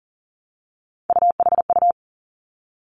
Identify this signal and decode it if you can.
Morse code, heard as a keyed tone, about 38 wpm, 700 Hz. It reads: UHU
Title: Three Challenging Letter-Number Combinations 40wpm